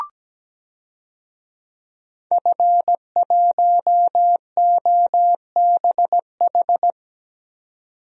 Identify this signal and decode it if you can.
Morse code, heard as a keyed tone, about 17 wpm, 700 Hz. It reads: F1OBH
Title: DXCC-100 Call Signs - 5 Characters - 17wpm